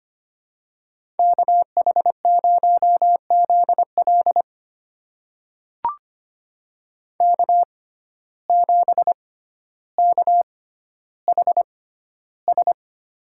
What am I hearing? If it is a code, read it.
Morse code, heard as a keyed tone, about 25 wpm, spaced as for 13 wpm, 700 Hz. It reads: KH0ZL K7KHS
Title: USA Call Signs - 5 Characters - 6x Character Spacing - 25wpm